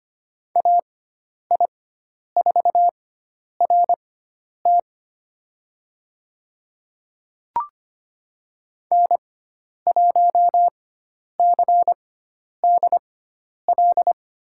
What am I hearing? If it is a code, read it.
Morse code, heard as a keyed tone, about 25 wpm, spaced as for 10 wpm, 700 Hz. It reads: AI4RT N1CDL
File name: Lesson 16g - Letter c - Call Signs 25wpm